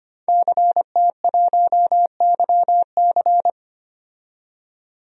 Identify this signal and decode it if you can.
Morse code, heard as a keyed tone, about 25 wpm, 700 Hz. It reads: CT1YC
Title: DXCC-100 Call Signs - 5 Characters - 25wpm